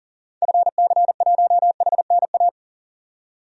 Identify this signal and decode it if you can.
Morse code, heard as a keyed tone, about 40 wpm, 700 Hz. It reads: FC1HNA
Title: DXCC-100 Call Signs - 6 Characters - 40wpm